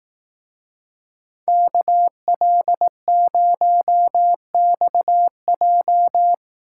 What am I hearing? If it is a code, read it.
Morse code, heard as a keyed tone, about 18 wpm, 700 Hz. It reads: KL0XJ
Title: Speed Racing - USA 2x2 Call Signs 15wpm